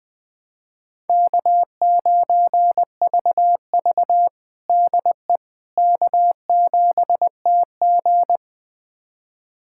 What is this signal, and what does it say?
Morse code, heard as a keyed tone, about 21 wpm, 700 Hz. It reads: K9VV DE K7TG
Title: USA Callsign DE Callsign - 4 Characters - 20wpm